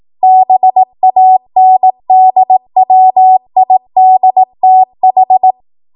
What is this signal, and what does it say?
Morse code, heard as a keyed tone, about 18 wpm, 750 Hz. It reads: BANDWIDTH
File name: Morse / 18WPM